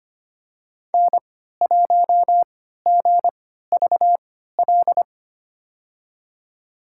Morse code, heard as a keyed tone, about 25 wpm, spaced as for 14 wpm, 700 Hz. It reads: N1GVL
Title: USA Call Signs - 5 Characters - 3x Character Spacing - 25wpm